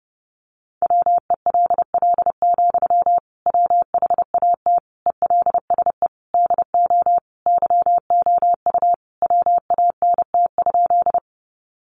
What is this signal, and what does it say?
Morse code, heard as a keyed tone, about 30 wpm, 700 Hz. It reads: WELL, WHAT ELSE DO YOU WANT?
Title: Sentences from Top 800 Words 30wpm